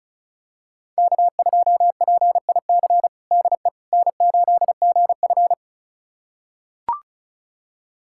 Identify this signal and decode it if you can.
Morse code, heard as a keyed tone, about 36 wpm, 700 Hz. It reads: K2PIC DE N8GF
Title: USA Callsign DE Callsign 35wpm